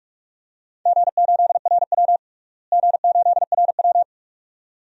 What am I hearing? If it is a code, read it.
Morse code, heard as a keyed tone, about 42 wpm, 700 Hz. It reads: G8RW G8RW
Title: DXCC-100 Call Signs - 4 Characters - Repeated 2x - 45wpm